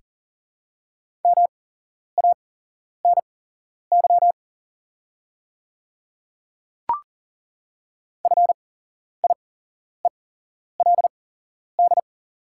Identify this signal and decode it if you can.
Morse code, heard as a keyed tone, about 40 wpm, spaced as for 11 wpm, 690 Hz. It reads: MANY FIELD
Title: 8x Character Spacing - Top 500 Words - 40wpm